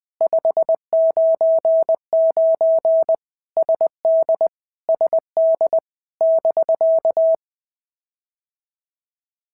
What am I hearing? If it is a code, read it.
Morse code, heard as a keyed tone, about 22 wpm, 650 Hz. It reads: 599 SD SD <BK>
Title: POTA QSO Practice - 20wpm